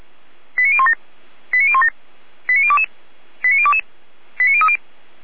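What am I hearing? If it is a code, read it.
Five-tone selective call (ZVEI), heard as quick runs of five tones, tones of about 80 ms. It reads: 89017 89017 89022 89022 89030
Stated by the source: ZVEI five-tone call received by SDR.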